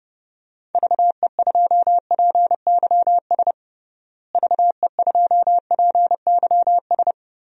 Morse code, heard as a keyed tone, about 32 wpm, 700 Hz. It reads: VE2PYS VE2PYS
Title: DXCC-100 Call Signs - 6 Characters - Repeated 3x - 30wpm